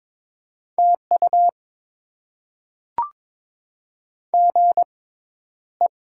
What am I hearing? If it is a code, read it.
Morse code, heard as a keyed tone, about 22 wpm, spaced as for 11 wpm, 700 Hz. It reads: TU GE
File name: Top 20 QSO Elements - 6x Character Spacing - 22wpm